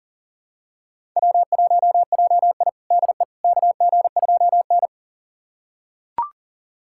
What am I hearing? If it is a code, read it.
Morse code, heard as a keyed tone, about 37 wpm, 700 Hz. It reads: W1JI DE KG2N
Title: USA Callsign DE Callsign - 4 Characters - 40wpm